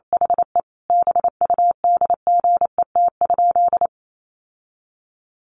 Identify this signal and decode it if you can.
Morse code, heard as a keyed tone, about 28 wpm, 700 Hz. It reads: HE BUDGET?
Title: Sentences from Top 1200 Words 28wpm